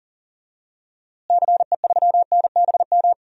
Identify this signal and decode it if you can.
Morse code, heard as a keyed tone, about 40 wpm, 700 Hz. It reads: CE3NBM
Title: DXCC-100 Call Signs - 6 Characters - Repeated 3x - 40wpm